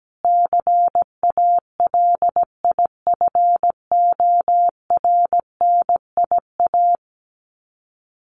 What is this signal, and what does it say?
Morse code, heard as a keyed tone, about 17 wpm, 700 Hz. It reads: CALIFORNIA
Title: Sets of 5 Words Encore2 17wpm